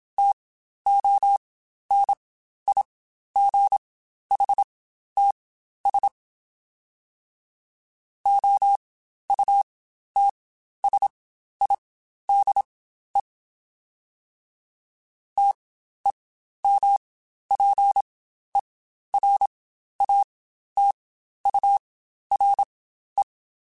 Morse code, heard as a keyed tone, about 27 wpm, spaced as for 12 wpm, 800 Hz. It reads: TONIGHTS OUTSIDE TEMPERATURE
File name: Spaced Characters / Spaced Characters-mp3s / Weather 5 22 sp @25